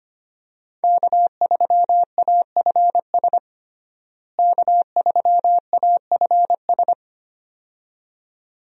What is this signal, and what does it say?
Morse code, heard as a keyed tone, about 25 wpm, 700 Hz. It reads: K3AFS K3AFS
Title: USA Call Signs - 5 Characters - Repeated 2x - 25wpm